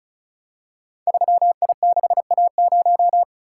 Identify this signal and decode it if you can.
Morse code, heard as a keyed tone, about 36 wpm, 700 Hz. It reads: 3I6A0
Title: Code Groups - 5 Letter:Numbers - 35wpm